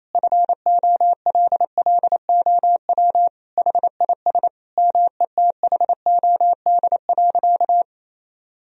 Morse code, heard as a keyed tone, about 28 wpm, 700 Hz. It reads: FOLLOW HIS METHOD.
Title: Sentences from Top 1000 Words - Review Part 3 - 28wpm